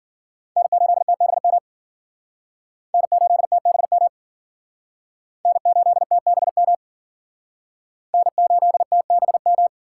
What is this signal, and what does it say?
Morse code, heard as a keyed tone, about 50 wpm, 700 Hz. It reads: N8TBM N8TBM N8TBM N8TBM